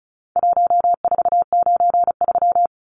Morse code, heard as a keyed tone, about 34 wpm, 700 Hz. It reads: 1493
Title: Numbers - Contest Style - 4-Digit 35wpm